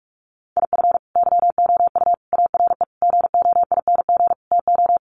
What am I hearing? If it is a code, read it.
Morse code, heard as a keyed tone, about 44 wpm, 700 Hz. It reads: IF YOU ARE GOING TO
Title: Sets of 5 Words Encore2 45wpm